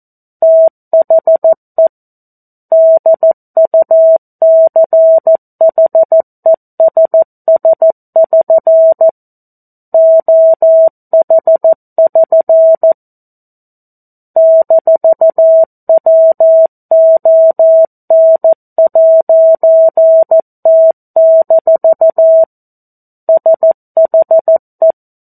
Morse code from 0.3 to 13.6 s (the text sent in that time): THE DUCHESS<SN> OH<SN>